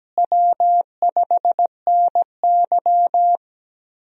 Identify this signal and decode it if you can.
Morse code, heard as a keyed tone, about 17 wpm, 700 Hz. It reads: W5NY
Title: USA Call Signs - 4 Characters - Repeated 3x - 17wpm